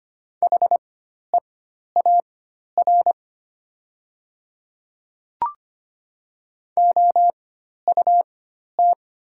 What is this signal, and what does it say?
Morse code, heard as a keyed tone, about 25 wpm, spaced as for 12 wpm, 700 Hz. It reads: HEAR OUT
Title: 4x Character Spacing - Top 300 Words - 25wpm